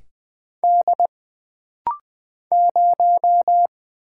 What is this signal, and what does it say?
Morse code, heard as a keyed tone, about 20 wpm, 700 Hz. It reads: D 0
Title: Single Character (Letter:Number:Punctuation:Prosign) - Rapid-Fire - 20wpm